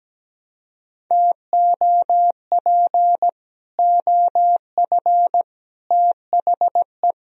Morse code, heard as a keyed tone, about 17 wpm, 700 Hz. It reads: TOP OF THE
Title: Speed-Racing - Sets of 3 Words 17wpm